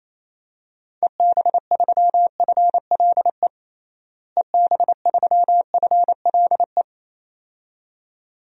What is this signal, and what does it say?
Morse code, heard as a keyed tone, about 28 wpm, 700 Hz. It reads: EB3FLE EB3FLE
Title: DXCC-100 Call Signs - 6 Characters - Repeated 2x - 28wpm